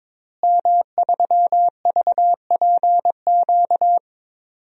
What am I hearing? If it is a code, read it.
Morse code, heard as a keyed tone, about 21 wpm, 700 Hz. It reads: M3VPQ
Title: DXCC-100 Call Signs - 5 Characters - Repeated 3x - 22wpm